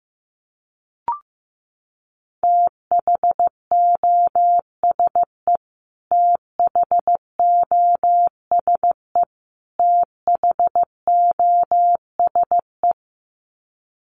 Morse code, heard as a keyed tone, about 15 wpm, 700 Hz. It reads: THOSE THOSE THOSE
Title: Top 300 Words Repeated 15wpm